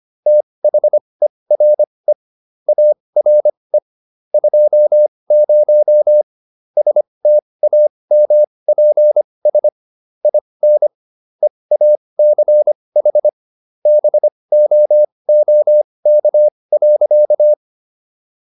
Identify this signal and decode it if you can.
Morse code, heard as a keyed tone, about 25 wpm, spaced as for 20 wpm, 600 Hz. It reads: THERE ARE 20 STAMPS IN EACH BOOK.